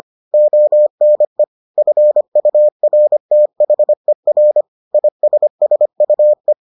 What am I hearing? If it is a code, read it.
Morse code, heard as a keyed tone, about 25 wpm, 600 Hz. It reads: ONE FURTHER ISSUE